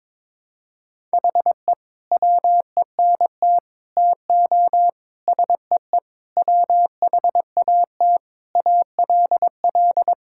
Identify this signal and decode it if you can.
Morse code, heard as a keyed tone, about 22 wpm, 700 Hz. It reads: HE WENT TO SEE WHAT ALL
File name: practice / 22wpm / Sentences with 8-Word Phrases - 22wpm